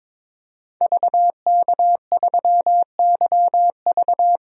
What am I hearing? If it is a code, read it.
Morse code, heard as a keyed tone, about 22 wpm, 700 Hz. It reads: VK3YV
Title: DXCC-100 Call Signs - 5 Characters - 22wpm